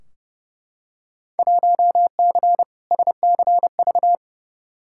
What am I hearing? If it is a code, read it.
Morse code, heard as a keyed tone, about 30 wpm, 700 Hz. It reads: 1C SCV
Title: Field Day Exchanges - 30wpm